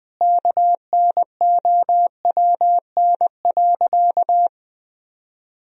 Morse code, heard as a keyed tone, about 20 wpm, 700 Hz. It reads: KNOWN.
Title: Sentences from Top 1000 Words - Review Part 4 - 20wpm